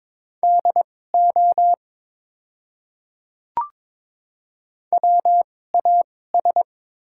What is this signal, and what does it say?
Morse code, heard as a keyed tone, about 22 wpm, spaced as for 16 wpm, 700 Hz. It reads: DO WAS
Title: 2x Character Spacing - Top 100 Words - 22wpm